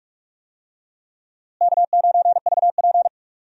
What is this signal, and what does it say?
Morse code, heard as a keyed tone, about 41 wpm, 700 Hz. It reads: K9VP
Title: USA Call Signs - 4 Characters - Repeated 2x - 45wpm